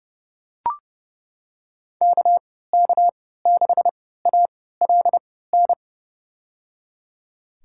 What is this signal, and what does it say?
Morse code, heard as a keyed tone, about 30 wpm, spaced as for 17 wpm, 700 Hz. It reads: KK6ALN
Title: USA Call Signs - 6 Characters - 3x Character Spacing - 30wpm